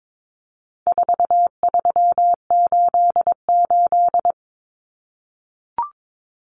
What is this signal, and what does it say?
Morse code, heard as a keyed tone, about 22 wpm, 700 Hz. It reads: 4388
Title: Common Cut Numbers - 4-Digit 22wpm